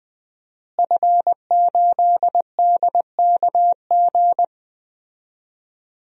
Morse code, heard as a keyed tone, about 20 wpm, 700 Hz. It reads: F8DKG
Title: DXCC-100 Call Signs - 5 Characters - Repeated 2x - 20wpm